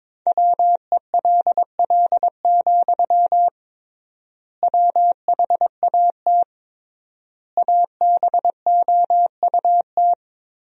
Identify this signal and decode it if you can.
Morse code, heard as a keyed tone, about 21 wpm, 700 Hz. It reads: WELL, WHAT ABOUT
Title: 3x Word-Spacing - Sentences from Top 500 Words - 22wpm